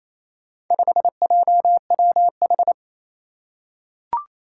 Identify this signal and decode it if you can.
Morse code, heard as a keyed tone, about 28 wpm, 700 Hz. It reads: 5JWH
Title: Four Challenging Letter-Number Combinations 28wpm